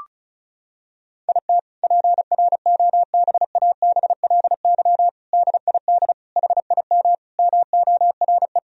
Morse code, heard as a keyed tone, about 36 wpm, 700 Hz. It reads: IT PROBABLY DID HIM MORE
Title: Sentences from Top 1000 Words - Review Part 4 - 35wpm